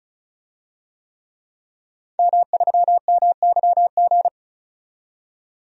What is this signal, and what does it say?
Morse code, heard as a keyed tone, about 36 wpm, 700 Hz. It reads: M3MYG
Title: DXCC-100 Call Signs - 5 Characters - 35wpm